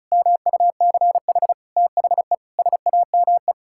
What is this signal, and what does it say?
Morse code, heard as a keyed tone, about 34 wpm, 700 Hz. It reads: MUCH THE SAME
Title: Sentences from Top 1600 Words 35wpm